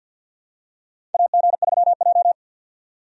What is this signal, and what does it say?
Morse code, heard as a keyed tone, about 47 wpm, 700 Hz. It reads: AG3J
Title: USA Call Signs - 4 Characters - 50wpm